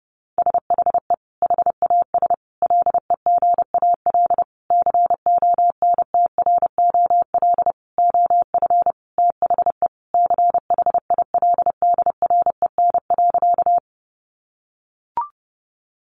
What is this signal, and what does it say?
Morse code, heard as a keyed tone, about 30 wpm, 700 Hz. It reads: SHE HAS LEGAL CONTROL OF THE CHILDREN.
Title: Sentences from Top 1200 Words 30wpm